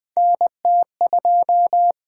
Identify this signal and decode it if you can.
Morse code, heard as a keyed tone, about 20 wpm, 700 Hz. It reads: NT2
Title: USA Call Signs - 4 Characters - 8x Character Spacing - 20wpm